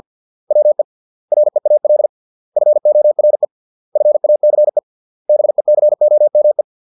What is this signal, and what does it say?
Morse code, heard as a keyed tone, about 50 wpm, 600 Hz. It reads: WE REAL FORE FACE BECOME